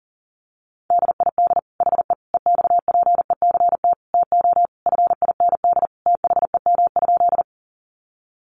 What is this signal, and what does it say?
Morse code, heard as a keyed tone, about 40 wpm, 700 Hz. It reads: DID HE EXPECT TO FIND THEM?
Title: Sentences from Top 1100 Words 40wpm